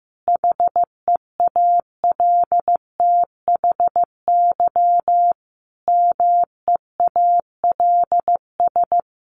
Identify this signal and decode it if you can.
Morse code, heard as a keyed tone, about 15 wpm, 700 Hz. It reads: HEALTHY MEALS
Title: Easy Phrases 15wpm